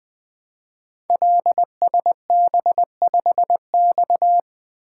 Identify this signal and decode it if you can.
Morse code, heard as a keyed tone, about 20 wpm, 700 Hz. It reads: LSB5X
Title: Code Groups - 5 Letter:Numbers - 20wpm